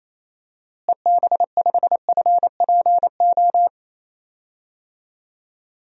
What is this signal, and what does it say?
Morse code, heard as a keyed tone, about 28 wpm, 700 Hz. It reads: EB5FPO
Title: DXCC-100 Call Signs - 6 Characters - Repeated 2x - 28wpm